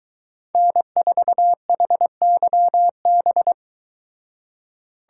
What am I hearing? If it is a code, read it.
Morse code, heard as a keyed tone, about 23 wpm, 700 Hz. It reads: N4HYB